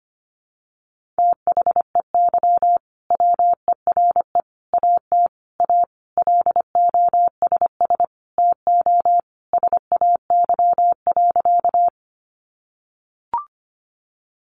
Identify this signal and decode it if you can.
Morse code, heard as a keyed tone, about 25 wpm, 700 Hz. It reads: THEY WERE AT A LOSS TO SAY.
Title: Sentences from Top 1400 Words 25wpm